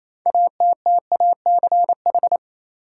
Morse code, heard as a keyed tone, about 27 wpm, 700 Hz. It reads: ATTACH
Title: Six Letter Words - 28wpm